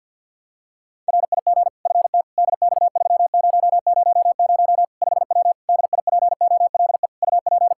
Morse code, heard as a keyed tone, about 53 wpm, 700 Hz. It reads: RIG FT DX3000 5W DIPOLE UP
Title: Practice QSO - Exchange 2 - Info - 50wpm